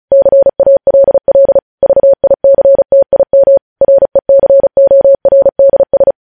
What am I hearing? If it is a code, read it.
Morse code, heard as a keyed tone, about 36 wpm, 550 Hz. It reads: CALL VICTIM RECORDS
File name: podcast / News headlines-1x-35x35